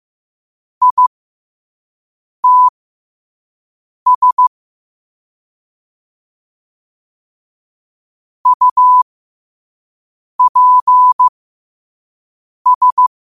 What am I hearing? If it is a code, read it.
Morse code, heard as a keyed tone, about 15 wpm, spaced as for 5 wpm, 1000 Hz. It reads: ITS UPS
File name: 5wpm